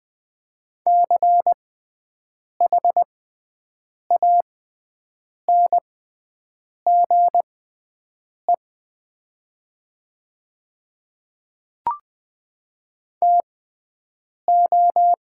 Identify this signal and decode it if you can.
Morse code, heard as a keyed tone, about 20 wpm, spaced as for 7 wpm, 690 Hz. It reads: CHANGE TO
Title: 6x Character Spacing - Top 300 Words - 20wpm